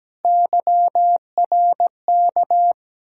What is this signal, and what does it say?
Morse code, heard as a keyed tone, about 17 wpm, 700 Hz. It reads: YRK
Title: Ontario QSO Party Exchange 17wpm